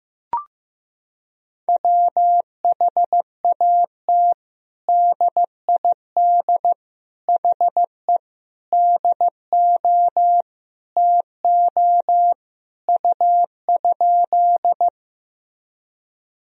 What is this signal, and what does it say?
Morse code, heard as a keyed tone, about 15 wpm, 700 Hz. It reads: WHAT DID HE DO TO U?
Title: Sentences from Top 100 Words and Abbreviations 15wpm